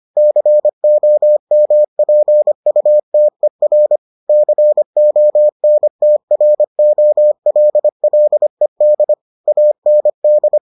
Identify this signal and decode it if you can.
Morse code, heard as a keyed tone, about 25 wpm, 600 Hz. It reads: COMPUTER CONTROLLED AND